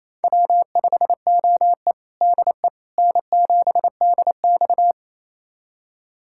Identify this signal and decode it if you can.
Morse code, heard as a keyed tone, about 27 wpm, 700 Hz. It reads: W5OE DE N7DX
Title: USA Callsign DE Callsign - 4 Characters - 28wpm